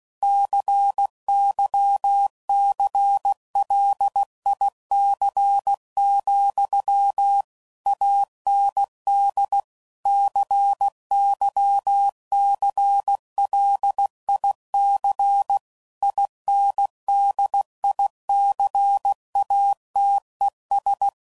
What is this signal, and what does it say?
Morse code, heard as a keyed tone, about 16 wpm, 800 Hz. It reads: CYCLIC, AND CYCLIC INDICATES
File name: CW Code Groups @15WPM / Technical Subjects / XC